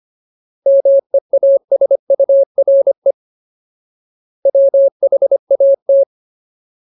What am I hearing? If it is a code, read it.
Morse code, heard as a keyed tone, about 25 wpm, 550 Hz. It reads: MEASURE WHAT
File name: Ultimate Word List / 25WPM Ultimate